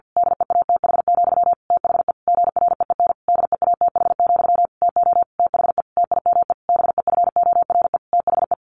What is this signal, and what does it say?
Morse code, heard as a keyed tone, about 50 wpm, 700 Hz. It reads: DEATH, THE GREEN DEATH, TO THE TIME BEFORE THE
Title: A Sound Of Thunder 50wpm